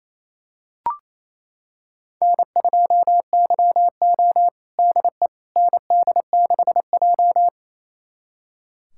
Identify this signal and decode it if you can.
Morse code, heard as a keyed tone, about 28 wpm, 700 Hz. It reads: N2YO DE ND6J
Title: USA Callsign DE Callsign - 4 Characters - 28wpm